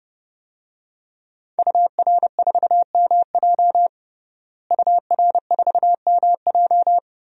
Morse code, heard as a keyed tone, about 30 wpm, 700 Hz. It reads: UR4MJ UR4MJ
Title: DXCC-100 Call Signs - 5 Characters - Repeated 2x - 30wpm